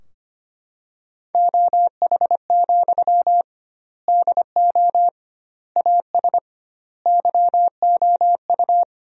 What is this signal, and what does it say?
Morse code, heard as a keyed tone, about 25 wpm, 700 Hz. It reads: OH, DO AS YOU
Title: 2x Word-Spacing - Sentences from Top 500 Words - 25wpm